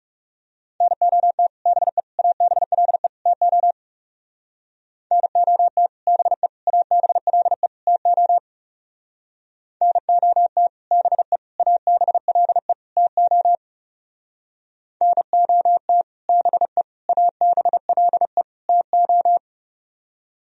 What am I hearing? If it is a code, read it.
Morse code, heard as a keyed tone, about 36 wpm, 700 Hz. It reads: NOT BE ABLE TO NOT BE ABLE TO NOT BE ABLE TO NOT BE ABLE TO
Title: Speed-Racing - Sets of 4 Words 30wpm